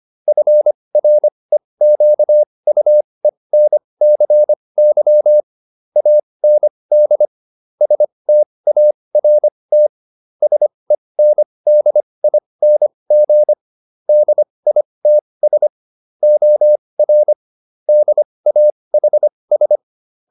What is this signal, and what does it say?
Morse code, heard as a keyed tone, about 25 wpm, spaced as for 20 wpm, 600 Hz. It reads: FREQUENCY AND START SENDING DITS OR DAHS
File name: SS209 20